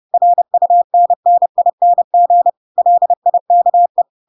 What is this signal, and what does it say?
Morse code, heard as a keyed tone, about 30 wpm, 700 Hz. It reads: RUNNING LIKE
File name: Pinocchio / Pinocchio ch3 30 30 ik1hge